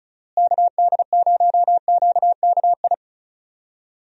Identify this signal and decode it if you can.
Morse code, heard as a keyed tone, about 36 wpm, 700 Hz. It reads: KD0QKI
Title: USA Call Signs - 6 Characters - Repeated 2x - 35wpm